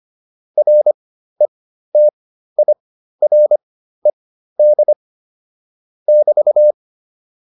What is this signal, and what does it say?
Morse code, heard as a keyed tone, about 25 wpm, spaced as for 13 wpm, 600 Hz. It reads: RETIRED <BT>
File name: QSO310 13